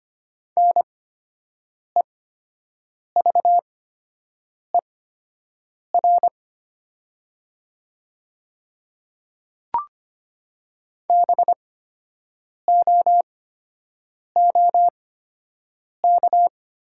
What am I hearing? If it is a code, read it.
Morse code, heard as a keyed tone, about 25 wpm, spaced as for 7 wpm, 700 Hz. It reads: NEVER BOOK
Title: 8x Character Spacing - Top 300 Words - 25wpm